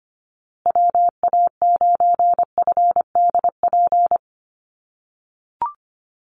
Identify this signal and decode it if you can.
Morse code, heard as a keyed tone, about 25 wpm, 700 Hz. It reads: WA9FDP
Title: USA Call Signs Repeated 2x 25wpm